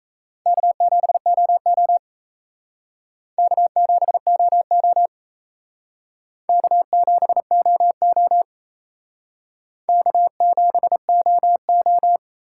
Morse code, heard as a keyed tone, about 38 wpm, 700 Hz. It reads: K7OO K7OO K7OO K7OO